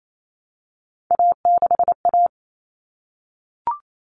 Morse code, heard as a keyed tone, about 28 wpm, 700 Hz. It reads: A6A
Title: Common Cut Numbers - 3-Digit 28wpm